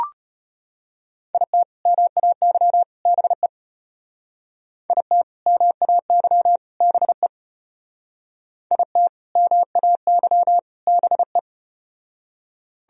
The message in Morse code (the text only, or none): IT MAY BE IT MAY BE IT MAY BE